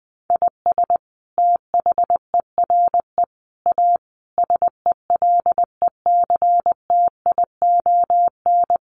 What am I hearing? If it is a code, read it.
Morse code, heard as a keyed tone, about 19 wpm, 700 Hz. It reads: IS THERE A SELECTION
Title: Sentences from Top 1400 Words 20wpm